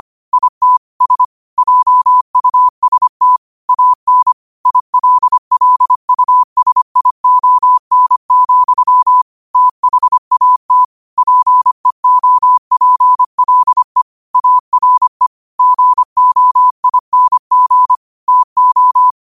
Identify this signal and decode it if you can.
Morse code, heard as a keyed tone, about 25 wpm, 1000 Hz. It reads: IT S JUST AN ILLUSION, THAT PEOPLE ARE GOING TO